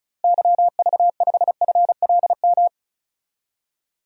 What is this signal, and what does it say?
Morse code, heard as a keyed tone, about 36 wpm, 700 Hz. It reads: YV5FLM
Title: DXCC-100 Call Signs - 6 Characters - Repeated 3x - 35wpm